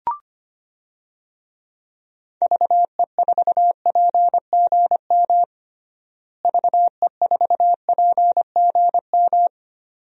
Morse code, heard as a keyed tone, about 25 wpm, 700 Hz. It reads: VE4PGM VE4PGM
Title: DXCC-100 Call Signs - 6 Characters - Repeated 2x - 25wpm